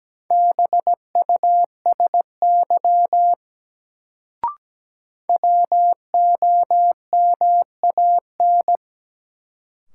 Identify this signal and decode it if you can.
Morse code, heard as a keyed tone, about 17 wpm, 700 Hz. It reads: BUSY WOMAN
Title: Sentences from Top 1000 Words - Review Part 2 - New Word Drill - 17wpm